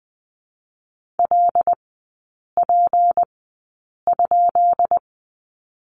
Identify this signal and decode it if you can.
Morse code, heard as a keyed tone, about 20 wpm, spaced as for 8 wpm, 700 Hz. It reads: LP?
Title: 166 Lesson 27e - Number 7 - Three Characters 20wpm